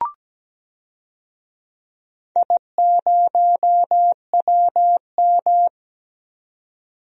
Morse code, heard as a keyed tone, about 17 wpm, 700 Hz. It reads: I0WM
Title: DXCC-100 Call Signs - 4 Characters - 17wpm